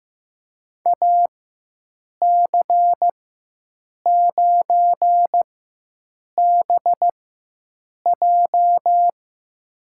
Morse code, heard as a keyed tone, about 15 wpm, spaced as for 7 wpm, 700 Hz. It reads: AC9BJ
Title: USA Call Signs - 5 Characters - 4x Character Spacing - 15wpm